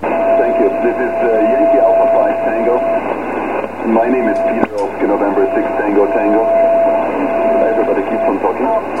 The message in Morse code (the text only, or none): none